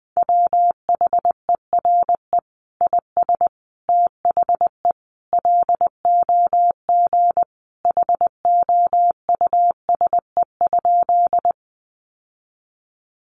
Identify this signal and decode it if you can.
Morse code, heard as a keyed tone, about 20 wpm, 700 Hz. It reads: WHERE IS THE LOG HOUSE?
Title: Sentences from Top 1400 Words 20wpm